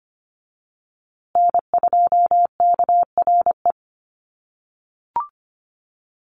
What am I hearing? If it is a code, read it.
Morse code, heard as a keyed tone, about 25 wpm, 700 Hz. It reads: N2KRE
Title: USA Call Signs Repeated 2x 25wpm